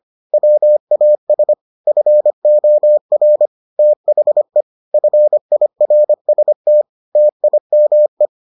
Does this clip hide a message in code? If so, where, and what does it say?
Yes: Morse code from 0.3 to 8.3 s, heard as a keyed tone, about 25 wpm, 600 Hz. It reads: WAS FOR THE FIRST TIME